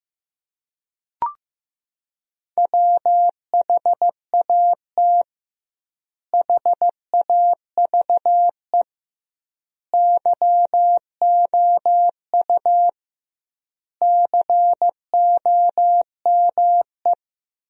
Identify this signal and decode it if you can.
Morse code, heard as a keyed tone, about 15 wpm, 700 Hz. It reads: WHAT HAVE YOU COME
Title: 2x Word-Spacing - Sentences from Top 100 Words - 15wpm